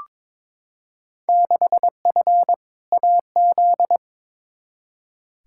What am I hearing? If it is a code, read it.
Morse code, heard as a keyed tone, about 22 wpm, 700 Hz. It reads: 6F AZ
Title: Field Day Exchanges Encore - 22wpm